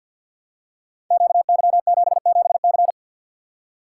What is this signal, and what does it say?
Morse code, heard as a keyed tone, about 50 wpm, 700 Hz. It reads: QY77C
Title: Code Groups - 5 Letter:Numbers - 50wpm